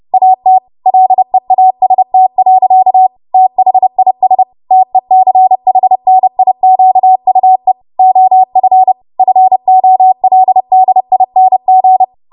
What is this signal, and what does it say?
Morse code, heard as a keyed tone, about 30 wpm, 750 Hz. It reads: AT LEAST. THIS TECHNIQUE OF FOLDING
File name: Morse / 30WPM